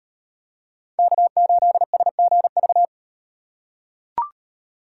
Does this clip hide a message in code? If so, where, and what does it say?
Yes: Morse code from 1.0 to 2.9 s, heard as a keyed tone, about 38 wpm, 700 Hz. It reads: K8SGV